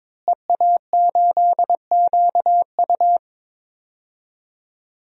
Morse code, heard as a keyed tone, about 22 wpm, 700 Hz. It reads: EA8QU
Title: DXCC-100 Call Signs - 5 Characters - Repeated 2x - 22wpm